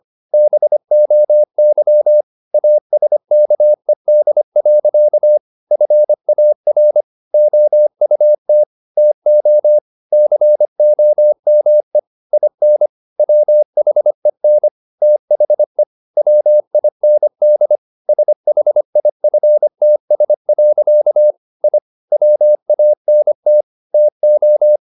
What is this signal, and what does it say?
Morse code, heard as a keyed tone, about 25 wpm, 600 Hz. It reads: BOY ASKED. FAR OUT TO COME IN WHEN THE WIND SHIFTS. I WANT TO
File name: the old man and the sea part 1 25wpm